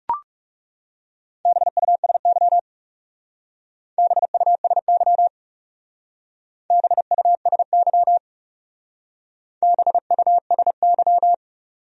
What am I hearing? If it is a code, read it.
Morse code, heard as a keyed tone, about 36 wpm, 700 Hz. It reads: BUSY BUSY BUSY BUSY
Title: Speed-Racing Top 500 Words 30wpm